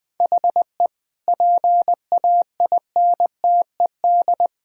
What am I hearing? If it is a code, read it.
Morse code, heard as a keyed tone, about 20 wpm, 700 Hz. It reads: HE PAINTED
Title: Sentences with 4-Word Phrases - Spoken Beginning - 20wpm